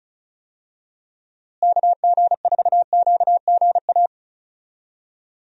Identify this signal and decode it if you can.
Morse code, heard as a keyed tone, about 34 wpm, 700 Hz. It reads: KG4QGA
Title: USA Call Signs - 6 Characters - Repeated 3x - 35wpm